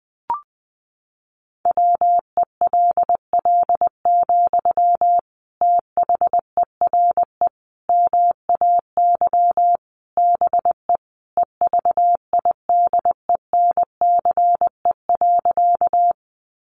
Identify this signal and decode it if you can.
Morse code, heard as a keyed tone, about 20 wpm, 700 Hz. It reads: WELL, THERE MAY BE EVIDENCE.
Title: Sentences from Top 1300 Words 20wpm